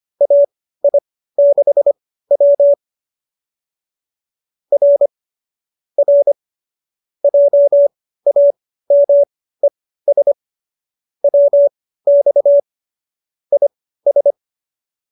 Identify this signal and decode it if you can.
Morse code, heard as a keyed tone, about 25 wpm, spaced as for 15 wpm, 570 Hz. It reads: AI6W R R JAMES WX IS